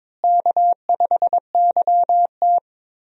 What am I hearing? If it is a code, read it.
Morse code, heard as a keyed tone, about 22 wpm, 700 Hz. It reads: K5YT
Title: USA Call Signs - 4 Characters - Repeated 2x - 22wpm